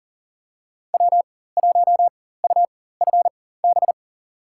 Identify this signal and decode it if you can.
Morse code, heard as a keyed tone, about 40 wpm, spaced as for 19 wpm, 700 Hz. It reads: W1UFB
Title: USA Call Signs - 5 Characters - 4x Character Spacing - 40wpm